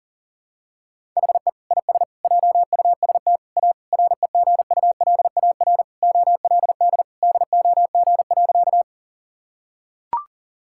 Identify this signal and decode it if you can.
Morse code, heard as a keyed tone, about 37 wpm, 700 Hz. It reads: HE IS JUST A REGULAR OLD DOG.
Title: Sentences from Top 1000 Words - Review Part 6 - 40wpm